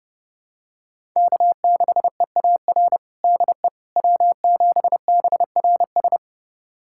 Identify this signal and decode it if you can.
Morse code, heard as a keyed tone, about 30 wpm, 700 Hz. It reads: K6EAR DE W7BRS
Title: USA Callsign DE Callsign - 5 Characters - 30wpm